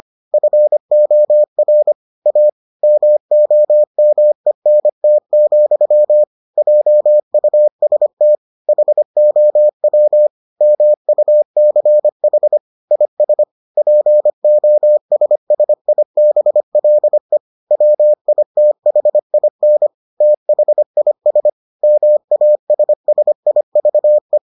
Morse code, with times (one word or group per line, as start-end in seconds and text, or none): 0.3-1.9 FOR
2.3-2.5 A
2.8-6.2 MOMENT,
6.6-8.3 JUST
8.7-10.3 HOW
10.6-12.6 MUCH
12.9-13.4 IS
13.8-17.4 POSSIBLE
17.7-19.9 WITHIN
20.2-21.5 THIS
21.8-24.4 MASSIVE